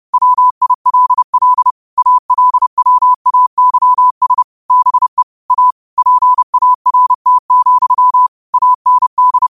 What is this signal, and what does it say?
Morse code, heard as a keyed tone, about 30 wpm, 1000 Hz. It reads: WILL ALWAYS BE A PART, AND
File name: qotdfiles / 30wpm